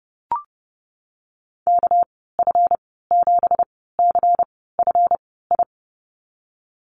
Morse code, heard as a keyed tone, about 30 wpm, spaced as for 17 wpm, 700 Hz. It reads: KF7CFI
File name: Lesson 37g - Letter x - Call Signs 30wpm